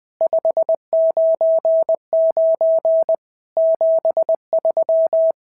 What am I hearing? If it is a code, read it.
Morse code, heard as a keyed tone, about 20 wpm, 650 Hz. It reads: 599 73
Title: SOTA QSO Practice 20wpm